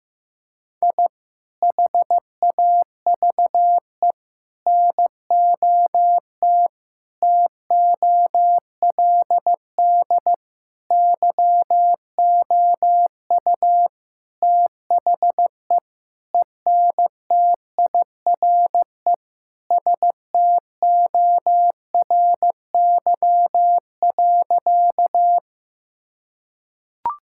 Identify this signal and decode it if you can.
Morse code, heard as a keyed tone, about 15 wpm, 700 Hz. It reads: I HAVE NOT TOLD YOU THE ENTIRE STORY.
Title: Sentences from Top 1000 Words - Review Part 5 - 15wpm